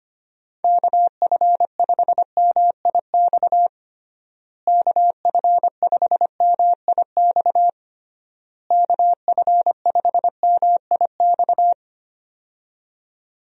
Morse code, heard as a keyed tone, about 25 wpm, 700 Hz. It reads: KF5MIX KF5MIX KF5MIX
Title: USA Call Signs - 6 Characters - Repeated 3x - 25wpm